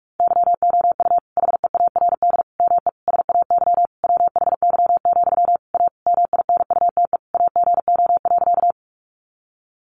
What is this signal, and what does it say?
Morse code, heard as a keyed tone, about 44 wpm, 700 Hz. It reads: YOU HEARD ME SAY WHY, A MINUTE AGO.
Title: Sentences from Top 500 Words 45wpm